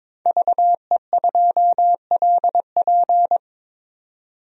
Morse code, heard as a keyed tone, about 22 wpm, 700 Hz. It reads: VE2LP
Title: DXCC-100 Call Signs - 5 Characters - 22wpm